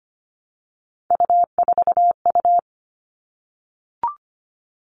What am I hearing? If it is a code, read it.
Morse code, heard as a keyed tone, about 25 wpm, 700 Hz. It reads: U4U
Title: All Cut Numbers - 3-Digit 25wpm